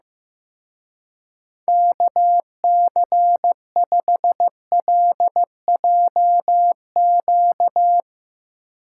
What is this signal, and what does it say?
Morse code, heard as a keyed tone, about 17 wpm, 700 Hz. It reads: KC5LJQ
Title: Speed Racing - USA 2x3 Call Signs 15wpm